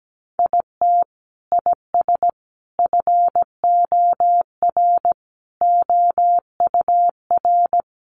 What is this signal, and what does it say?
Morse code, heard as a keyed tone, about 17 wpm, 700 Hz. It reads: IT IS FOR OUR
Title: Sentences from Top 700 Words 17wpm